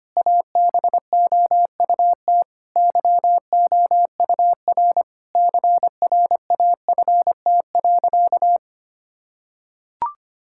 Morse code, heard as a keyed tone, about 25 wpm, 700 Hz. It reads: ABOUT YOUR CRAFT.
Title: Sentences from Top 1000 Words - Review Part 6 - 25wpm